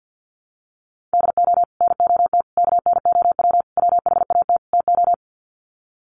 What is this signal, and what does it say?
Morse code, heard as a keyed tone, about 50 wpm, 700 Hz. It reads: DO NOT KNOW WHAT TO
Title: Sets of 5 Words 50wpm